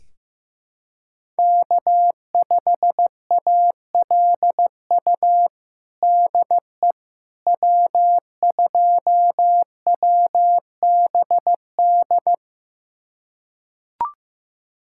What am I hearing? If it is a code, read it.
Morse code, heard as a keyed tone, about 15 wpm, 700 Hz. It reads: K5ALU DE W2WBD
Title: USA Callsign DE Callsign - 5 Characters - 15wpm